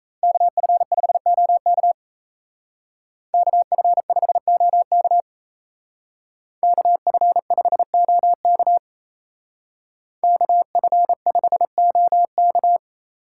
Morse code, heard as a keyed tone, about 33 wpm, 700 Hz. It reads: KF5OK KF5OK KF5OK KF5OK
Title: Speed Racing - USA 2x2 Call Signs 28wpm